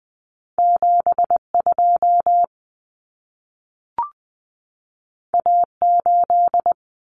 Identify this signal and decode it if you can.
Morse code, heard as a keyed tone, about 20 wpm, 700 Hz. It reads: 72 A8
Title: Common Cut Numbers - 2-Digit 20wpm